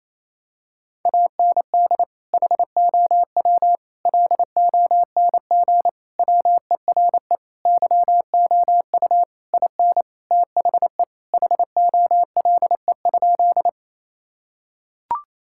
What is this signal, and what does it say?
Morse code, heard as a keyed tone, about 28 wpm, 700 Hz. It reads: AND HOW LONG WERE YOU IN THE HOLE?
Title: Sentences from Top 1600 Words 28wpm